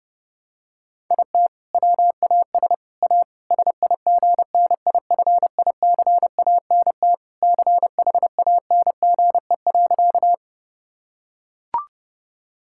Morse code, heard as a keyed tone, about 30 wpm, 700 Hz. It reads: IT WAS A SIGNIFICANT CHANGE.
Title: Sentences from Top 1000 Words - Review Part 6 - 30wpm